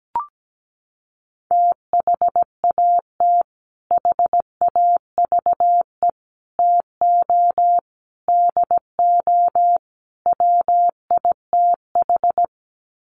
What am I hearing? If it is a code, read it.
Morse code, heard as a keyed tone, about 17 wpm, 700 Hz. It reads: THAT HAVE TO DO WITH
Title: Sets of 5 Words Encore2 17wpm